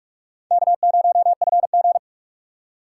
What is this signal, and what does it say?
Morse code, heard as a keyed tone, about 41 wpm, 700 Hz. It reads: K0FG
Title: USA Call Signs - 4 Characters - 3x Character Spacing - 45wpm